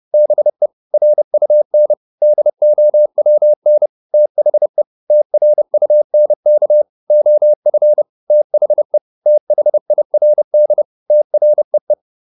Morse code, heard as a keyed tone, about 30 wpm, 600 Hz. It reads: BE RUN DOWN THE TRUNK OF THE THIRD TREE
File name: SS302 30